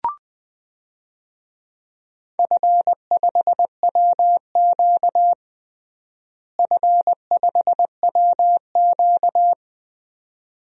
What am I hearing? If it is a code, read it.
Morse code, heard as a keyed tone, about 20 wpm, 700 Hz. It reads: F5WQ F5WQ
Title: DXCC-100 Call Signs - 4 Characters - Repeated 2x - 20wpm